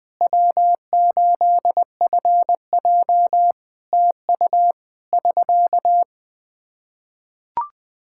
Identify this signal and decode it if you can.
Morse code, heard as a keyed tone, about 20 wpm, 700 Hz. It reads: W8FJ TU <SK>
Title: Practice QSO - Exchange 4 - Ending - 20wpm